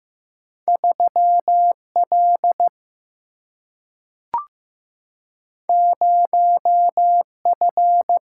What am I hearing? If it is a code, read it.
Morse code, heard as a keyed tone, about 15 wpm, 700 Hz. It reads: 3L 0F
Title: Single-Letter Single-Number 15wpm